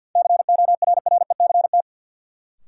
Morse code, heard as a keyed tone, about 51 wpm, 700 Hz. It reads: CORRECT
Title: Sentences from Top 700 Words - New Word Drill - 50wpm